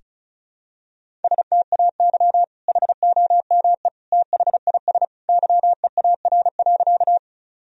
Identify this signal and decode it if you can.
Morse code, heard as a keyed tone, about 36 wpm, 700 Hz. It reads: STAY HOME THIS YEAR.
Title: Sentences from Top 1000 Words - Review Part 1 - 35wpm